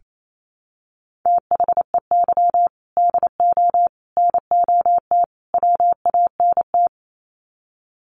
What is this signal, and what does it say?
Morse code, heard as a keyed tone, about 28 wpm, 700 Hz. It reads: THEY DO NOT WANT
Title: Sets of 4 Words 28wpm